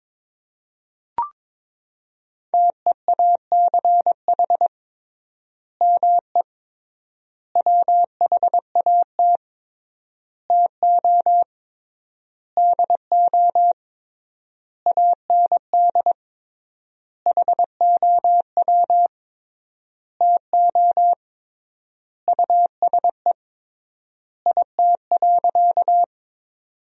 Morse code, heard as a keyed tone, about 22 wpm, 700 Hz. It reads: TEACH ME WHAT TO DO AND HOW TO USE IT.
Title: 3x Word-Spacing - Sentences from Top 500 Words - 22wpm